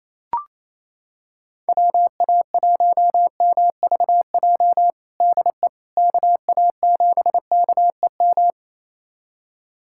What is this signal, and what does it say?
Morse code, heard as a keyed tone, about 28 wpm, 700 Hz. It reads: WA1MVJ DE KA7KEM
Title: USA Callsign DE Callsign - 6 Characters - 28wpm